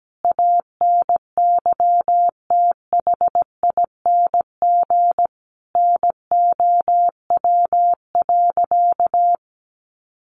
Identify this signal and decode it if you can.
Morse code, heard as a keyed tone, about 17 wpm, 700 Hz. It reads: ANYTHING NOW.
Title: Sentences from Top 1400 Words 17wpm